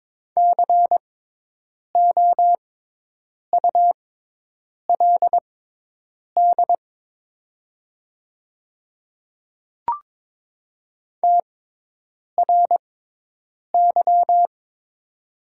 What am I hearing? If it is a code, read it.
Morse code, heard as a keyed tone, about 22 wpm, spaced as for 8 wpm, 690 Hz. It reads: COULD TRY
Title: 6x Character Spacing - Top 200 Words - 22wpm